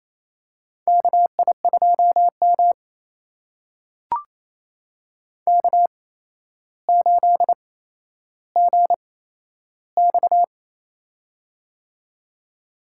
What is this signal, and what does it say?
Morse code, heard as a keyed tone, about 28 wpm, spaced as for 12 wpm, 700 Hz. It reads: KI2M K8GX
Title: USA Call Signs - 4 Characters - 8x Character Spacing - 28wpm